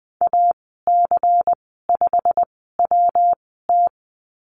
Morse code, heard as a keyed tone, about 20 wpm, spaced as for 14 wpm, 700 Hz. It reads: AC5WT
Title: 122 Lesson 20g - Number 3 - Call Signs 20wpm